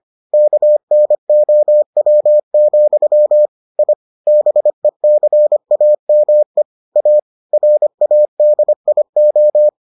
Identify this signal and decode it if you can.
Morse code, heard as a keyed tone, about 25 wpm, 600 Hz. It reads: KNOW, I BECAME A RADIO